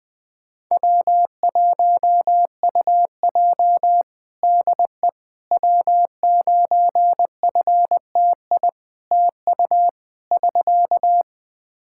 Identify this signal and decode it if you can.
Morse code, heard as a keyed tone, about 20 wpm, 700 Hz. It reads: W1UJ DE W9FTI TU <SK>
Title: Practice QSO - Exchange 4 - Ending - 20wpm